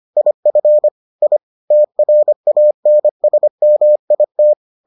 Morse code, heard as a keyed tone, about 25 wpm, 600 Hz. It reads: IF I TRANSMIT